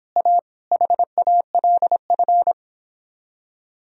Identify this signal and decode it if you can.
Morse code, heard as a keyed tone, about 29 wpm, 700 Hz. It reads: A HALF